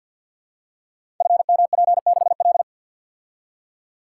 Morse code, heard as a keyed tone, about 52 wpm, 700 Hz. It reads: FMP6L
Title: Code Groups - 5 Letter:Numbers - 50wpm